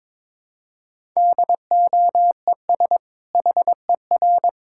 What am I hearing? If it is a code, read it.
Morse code, heard as a keyed tone, about 22 wpm, 700 Hz. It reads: DOES HER
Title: Sentences from Top 1000 Words - Review Part 5 - 22wpm